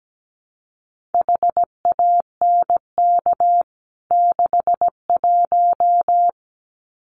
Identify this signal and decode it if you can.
Morse code, heard as a keyed tone, about 17 wpm, 700 Hz. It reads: HANK 61
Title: CWT Exchange 17wpm